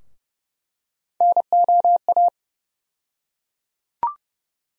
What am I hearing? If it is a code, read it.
Morse code, heard as a keyed tone, about 30 wpm, 700 Hz. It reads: NOA
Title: USA Call Sign Postfixes - 3 Letters 30wpm